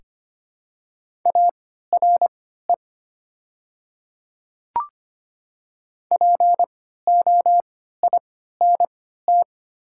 Morse code, heard as a keyed tone, about 25 wpm, spaced as for 14 wpm, 700 Hz. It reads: ARE POINT
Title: 3x Character Spacing - Top 200 Words - 25wpm